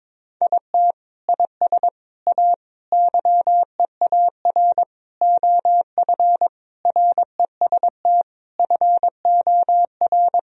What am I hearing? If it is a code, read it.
Morse code, heard as a keyed tone, about 22 wpm, 700 Hz. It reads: IT IS A YEAR OF REST FOR
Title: Sentences from Top 1000 Words - Review Part 1 - 22wpm